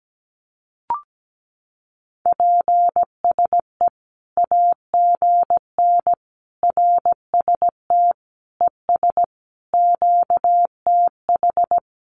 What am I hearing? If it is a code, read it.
Morse code, heard as a keyed tone, about 17 wpm, 700 Hz. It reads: PSE AGN RST ES QTH
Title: Common QSO Phrases 17wpm